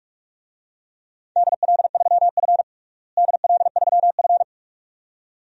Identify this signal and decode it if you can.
Morse code, heard as a keyed tone, about 46 wpm, 700 Hz. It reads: DL3F DL3F
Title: DXCC-100 Call Signs - 4 Characters - Repeated 2x - 45wpm